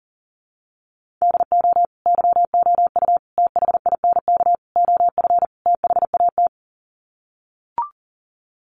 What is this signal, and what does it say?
Morse code, heard as a keyed tone, about 40 wpm, 700 Hz. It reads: DO YOU THINK OF THAT
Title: Sets of 5 Words Encore2 40wpm